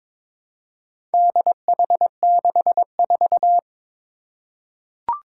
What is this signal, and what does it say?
Morse code, heard as a keyed tone, about 22 wpm, 700 Hz. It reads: DH64
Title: Four Challenging Letter-Number Combinations 22wpm